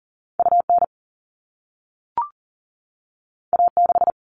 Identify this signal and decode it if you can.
Morse code, heard as a keyed tone, about 40 wpm, 700 Hz. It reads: UN A6
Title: All Cut Numbers - 2-Digit 40wpm